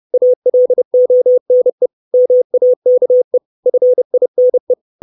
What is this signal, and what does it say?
Morse code, heard as a keyed tone, about 30 wpm, 500 Hz. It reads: ALONE MAKE FINE
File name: 632 Words 30WPM